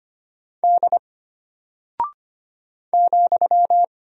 Morse code, heard as a keyed tone, about 25 wpm, 700 Hz. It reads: D ,
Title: Single Character (Letter:Number:Punctuation:Prosign) Alt 25wpm